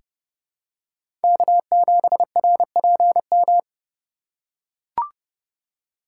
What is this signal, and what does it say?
Morse code, heard as a keyed tone, about 30 wpm, 700 Hz. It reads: K7RPM
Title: USA Call Signs - 4x Character Spacing - 30wpm